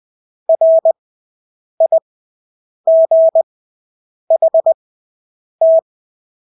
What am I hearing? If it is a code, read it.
Morse code, heard as a keyed tone, about 20 wpm, spaced as for 8 wpm, 650 Hz. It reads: RIGHT
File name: Engelske ord 20/8 wpm